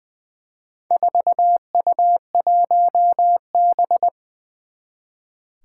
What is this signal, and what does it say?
Morse code, heard as a keyed tone, about 20 wpm, 700 Hz. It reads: 4U1B
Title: Four Challenging Letter-Number Combinations 20wpm